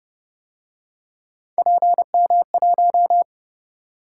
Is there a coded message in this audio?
Yes: Morse code from 1.6 to 3.2 s, heard as a keyed tone, about 31 wpm, 700 Hz. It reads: PM1